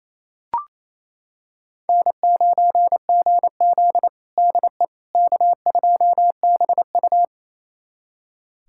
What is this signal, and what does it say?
Morse code, heard as a keyed tone, about 28 wpm, 700 Hz. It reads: N9GZ DE K2BU
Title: USA Callsign DE Callsign - 4 Characters - 28wpm